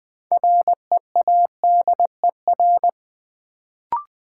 Morse code, heard as a keyed tone, about 20 wpm, 700 Hz. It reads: READER
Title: Build A Word - 3 Letters At A Time - 6-Letter Words - 20wpm